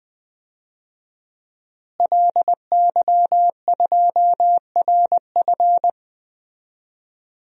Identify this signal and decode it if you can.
Morse code, heard as a keyed tone, about 20 wpm, 700 Hz. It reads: LY2RF
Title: DXCC-100 Call Signs - 5 Characters - Repeated 2x - 20wpm